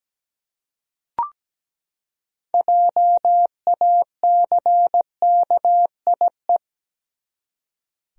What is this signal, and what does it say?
Morse code, heard as a keyed tone, about 17 wpm, 700 Hz. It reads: JACKIE
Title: USA Common Names - 17wpm